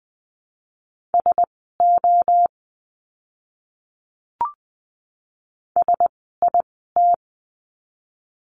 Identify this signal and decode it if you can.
Morse code, heard as a keyed tone, about 20 wpm, spaced as for 15 wpm, 700 Hz. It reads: SO SIT
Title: 108 Lesson 18f - Letter m - Words 20wpm